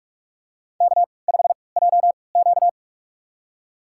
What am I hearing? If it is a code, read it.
Morse code, heard as a keyed tone, about 46 wpm, spaced as for 26 wpm, 700 Hz. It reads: K5JQ
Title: USA Call Signs - 4 Characters - 3x Character Spacing - 45wpm